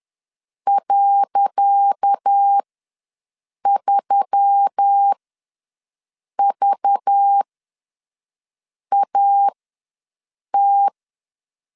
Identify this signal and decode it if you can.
Morse code, heard as a keyed tone, about 10 wpm, spaced as for 5 wpm, 800 Hz. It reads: .3VAT
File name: LEZIONI / 10 SIGNS / 10C DOT ws 10wpm 4213